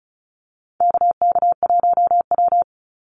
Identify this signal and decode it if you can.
Morse code, heard as a keyed tone, about 36 wpm, 700 Hz. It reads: KK1W
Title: NA SOTA Activator Callsigns 35wpm